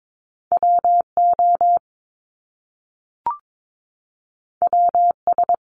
Morse code, heard as a keyed tone, about 22 wpm, 700 Hz. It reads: W O W S
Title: Top Two Letter Combinations in Words 22wpm